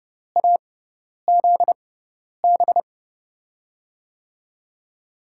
Morse code, heard as a keyed tone, about 29 wpm, spaced as for 10 wpm, 700 Hz. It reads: AZB
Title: USA Call Signs - 6x Character Spacing - 30wpm